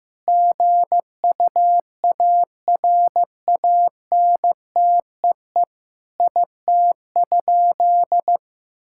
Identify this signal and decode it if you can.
Morse code, heard as a keyed tone, about 15 wpm, 700 Hz. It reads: GUARANTEE IT?
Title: Sentences from Top 1600 Words 15wpm